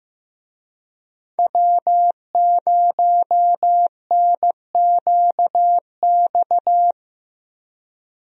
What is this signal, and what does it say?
Morse code, heard as a keyed tone, about 15 wpm, 700 Hz. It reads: W0NQX
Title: USA Call Signs - 5 Characters - Repeated 2x - 15wpm